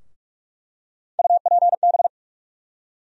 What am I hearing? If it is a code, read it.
Morse code, heard as a keyed tone, about 45 wpm, 700 Hz. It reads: UPB
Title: USA Call Sign Postfixes - 3 Letters 45wpm